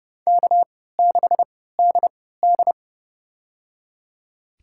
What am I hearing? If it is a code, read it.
Morse code, heard as a keyed tone, about 30 wpm, spaced as for 17 wpm, 700 Hz. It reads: K6DD
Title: USA Call Signs - 4 Characters - 3x Character Spacing - 30wpm